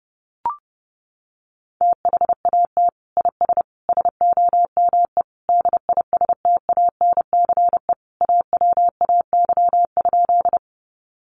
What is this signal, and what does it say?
Morse code, heard as a keyed tone, about 30 wpm, 700 Hz. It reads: THAT IS SOME DISTANCE AWAY?
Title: Sentences from Top 1200 Words 30wpm